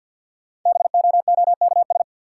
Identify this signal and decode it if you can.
Morse code, heard as a keyed tone, about 50 wpm, 700 Hz. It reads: BOOKS
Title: Sentences from Top 1200 Words - New Word Drill Alt - 50wpm